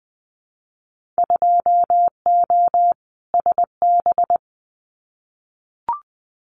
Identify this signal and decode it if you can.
Morse code, heard as a keyed tone, about 20 wpm, 700 Hz. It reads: 2O SB
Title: Winter Field Day Exchanges 20wpm